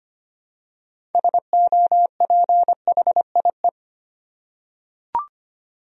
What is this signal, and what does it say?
Morse code, heard as a keyed tone, about 25 wpm, 700 Hz. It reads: SOPHIE
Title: USA Common Names - 25wpm